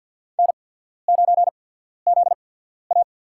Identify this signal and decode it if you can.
Morse code, heard as a keyed tone, about 52 wpm, 690 Hz. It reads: N 9 Z A
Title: USA Call Signs - 4 Characters - 8x Character Spacing - 50wpm